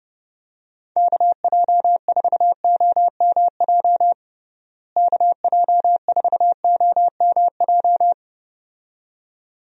Morse code, heard as a keyed tone, about 30 wpm, 700 Hz. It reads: KJ4OMJ KJ4OMJ
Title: USA Call Signs - 6 Characters - Repeated 2x - 30wpm